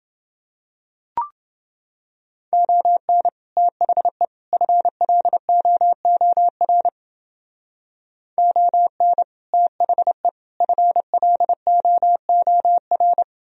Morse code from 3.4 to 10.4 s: THE FLOOR ON THE

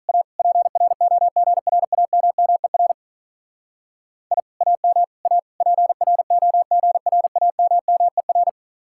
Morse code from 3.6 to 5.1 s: I AM

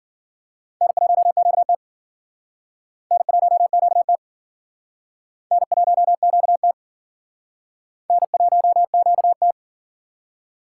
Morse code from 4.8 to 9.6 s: N1QT N1QT

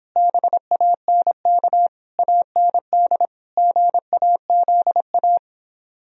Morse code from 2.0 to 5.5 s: AND GAZA

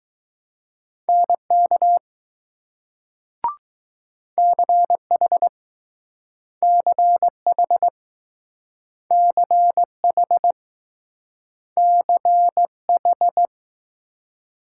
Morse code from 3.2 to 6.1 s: CH

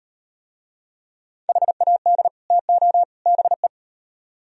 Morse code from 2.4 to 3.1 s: TO

